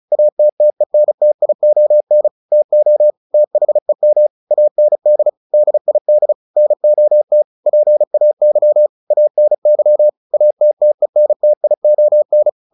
Morse code, heard as a keyed tone, about 34 wpm, 600 Hz. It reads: ATTENTION TO THEM AND DID NOT PAY ANY ATTENTION
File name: the old man and the sea 2 35wpm